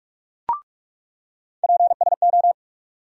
Morse code, heard as a keyed tone, about 46 wpm, 700 Hz. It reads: PSO
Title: USA Call Sign Postfixes - 3 Letters 45wpm